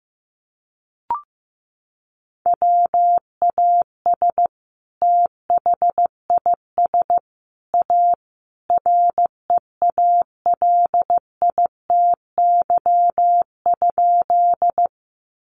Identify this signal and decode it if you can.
Morse code, heard as a keyed tone, about 15 wpm, 700 Hz. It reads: WAS THIS A REALITY?
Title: Sentences from Top 1200 Words 15wpm